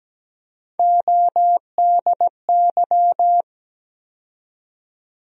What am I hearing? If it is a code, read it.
Morse code, heard as a keyed tone, about 17 wpm, 700 Hz. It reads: ODY
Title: USA Call Sign Postfixes - 3 Letters 17wpm